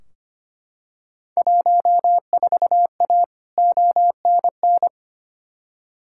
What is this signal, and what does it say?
Morse code, heard as a keyed tone, about 25 wpm, 700 Hz. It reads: 14A ONN
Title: Field Day Exchanges Encore - 25wpm